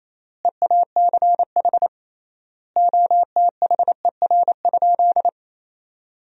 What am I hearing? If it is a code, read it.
Morse code, heard as a keyed tone, about 28 wpm, 700 Hz. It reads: EACH OTHER?
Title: 3x Word-Spacing - Sentences from Top 200 Words - 28wpm